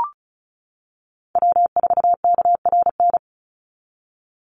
Morse code, heard as a keyed tone, about 36 wpm, 700 Hz. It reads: W4KRN
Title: NA SOTA Chaser Callsigns 35wpm